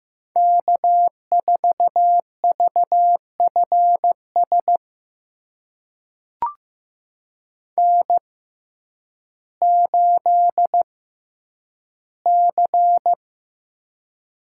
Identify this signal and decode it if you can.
Morse code, heard as a keyed tone, about 15 wpm, spaced as for 9 wpm, 700 Hz. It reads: K4VFS N8C
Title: USA Call Signs - 5 Characters - 6x Character Spacing - 15wpm